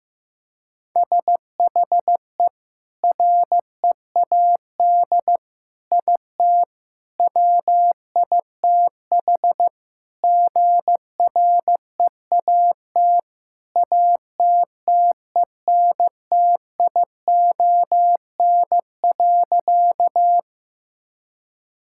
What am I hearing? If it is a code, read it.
Morse code, heard as a keyed tone, about 15 wpm, 700 Hz. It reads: SHE READ IT WITH GREAT ATTENTION.
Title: Sentences from Top 1000 Words - Review Part 6 - 15wpm